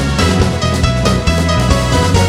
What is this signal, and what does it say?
There is no code.